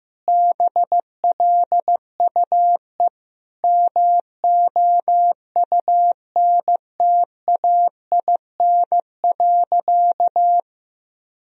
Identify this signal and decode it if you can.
Morse code, heard as a keyed tone, about 15 wpm, 700 Hz. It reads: BLUE MOUNTAIN.
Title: Sentences from Top 1000 Words - Review Part 3 - 15wpm